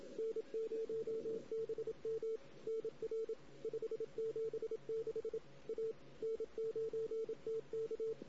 Morse code, heard as a keyed tone, about 27 wpm, 450 Hz. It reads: N0BM NR 576 A N9TK
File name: CW / 2008-ss-cw-k5tr-op-n5rz -16:00